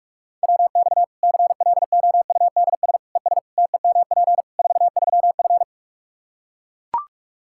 Morse code, heard as a keyed tone, about 41 wpm, 700 Hz. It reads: WX CLOUDS ES TEMP 43F
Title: K1USN SST QSO Practice 45wpm